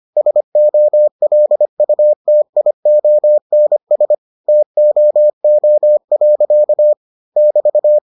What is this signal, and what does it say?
Morse code, heard as a keyed tone, about 25 wpm, 600 Hz. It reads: SOLUTIONS TOO. <BT>